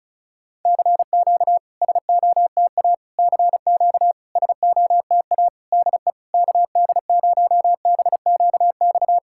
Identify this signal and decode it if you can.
Morse code, heard as a keyed tone, about 34 wpm, 690 Hz. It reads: CQ SOTA CQ SOTA DE KD0BQX
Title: SOTA QSO Practice 35wpm